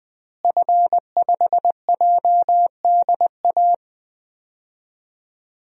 Morse code, heard as a keyed tone, about 20 wpm, 700 Hz. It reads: F5JDA
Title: DXCC-100 Call Signs - 5 Characters - Repeated 3x - 20wpm